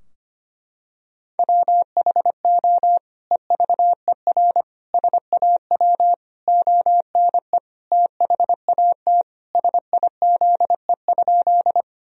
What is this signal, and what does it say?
Morse code, heard as a keyed tone, about 25 wpm, 700 Hz. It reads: WHO EVER SAW ONE THAT SIZE?
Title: Sentences from Top 1000 Words - Review Part 2 - 25wpm